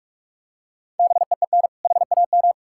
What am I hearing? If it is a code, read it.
Morse code, heard as a keyed tone, about 46 wpm, 700 Hz. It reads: BEEN HAM
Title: Practice QSO - Exchange 3 - More Info - 45wpm